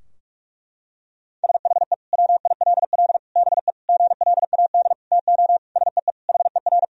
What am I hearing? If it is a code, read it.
Morse code, heard as a keyed tone, about 46 wpm, 700 Hz. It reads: SHE WILL BE GLAD TO SEE HER
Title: Sentences from Top 1000 Words - Review Part 2 - 45wpm